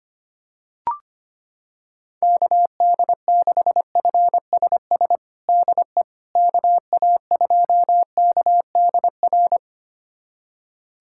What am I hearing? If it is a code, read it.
Morse code, heard as a keyed tone, about 25 wpm, 700 Hz. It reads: KD6FSS DE KA2KDR
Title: USA Callsign DE Callsign 25wpm